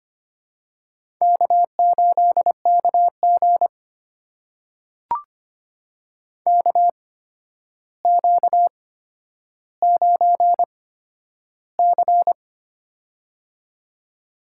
Morse code, heard as a keyed tone, about 25 wpm, spaced as for 11 wpm, 700 Hz. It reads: K8KG KQ9C
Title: USA Call Signs - 4 Characters - 8x Character Spacing - 25wpm